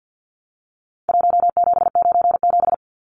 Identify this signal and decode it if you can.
Morse code, heard as a keyed tone, about 49 wpm, 700 Hz. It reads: 1797
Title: Numbers - Contest Style - 4-Digit 50wpm